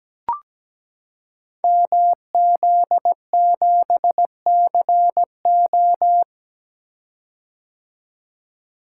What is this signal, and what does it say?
Morse code, heard as a keyed tone, about 17 wpm, 700 Hz. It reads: MZ7CO
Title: Code Groups - 5 Letter:Numbers - 17wpm